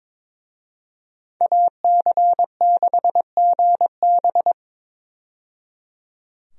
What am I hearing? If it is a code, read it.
Morse code, heard as a keyed tone, about 22 wpm, 700 Hz. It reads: AC6GB
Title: USA Call Signs - 5 Characters - 22wpm